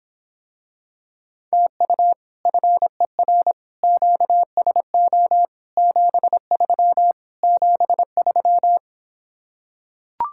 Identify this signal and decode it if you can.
Morse code, heard as a keyed tone, about 26 wpm, 700 Hz. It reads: TU FER QSO 73 73